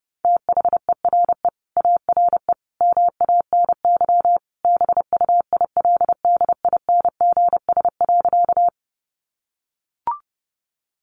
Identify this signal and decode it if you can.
Morse code, heard as a keyed tone, about 30 wpm, 700 Hz. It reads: THERE ARE MANY BUILDINGS.
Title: Easy Phrases 30wpm